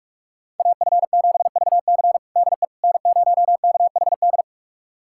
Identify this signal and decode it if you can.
Morse code, heard as a keyed tone, about 41 wpm, 700 Hz. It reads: AF7VC DE N0KHD
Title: USA Callsign DE Callsign - 5 Characters - 45wpm